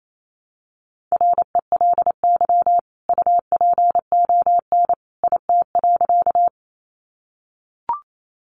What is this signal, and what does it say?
Morse code, heard as a keyed tone, about 28 wpm, 700 Hz. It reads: RELY UPON IT.
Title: Sentences from Top 1200 Words 28wpm